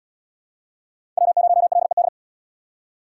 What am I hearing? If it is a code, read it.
Morse code, heard as a keyed tone, about 76 wpm, 700 Hz. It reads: W0LED